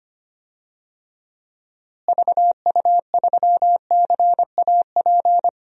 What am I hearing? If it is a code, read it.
Morse code, heard as a keyed tone, about 25 wpm, 700 Hz. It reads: VU3CAP
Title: DXCC-100 Call Signs - 6 Characters - Repeated 3x - 25wpm